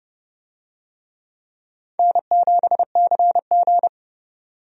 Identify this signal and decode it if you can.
Morse code, heard as a keyed tone, about 30 wpm, 700 Hz. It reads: N7CG
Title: USA Call Signs - 4 Characters - 30wpm